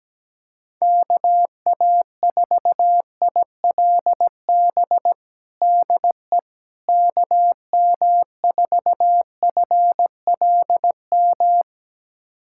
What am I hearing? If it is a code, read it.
Morse code, heard as a keyed tone, about 17 wpm, 700 Hz. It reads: KA4ILB DE KM4FLM
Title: USA Callsign DE Callsign 17wpm